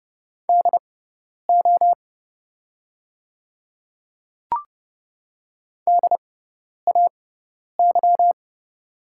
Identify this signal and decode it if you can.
Morse code, heard as a keyed tone, about 29 wpm, spaced as for 10 wpm, 700 Hz. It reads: DO DAY
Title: 6x Character Spacing - Top 100 Words - 30wpm